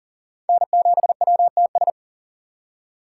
Morse code, heard as a keyed tone, about 40 wpm, 700 Hz. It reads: N7WTS
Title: USA Call Signs - 2x Character Spacing - 40wpm